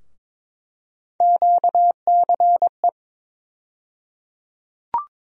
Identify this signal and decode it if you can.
Morse code, heard as a keyed tone, about 22 wpm, 700 Hz. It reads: QCE
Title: USA Call Sign Postfixes - 3 Letters 22wpm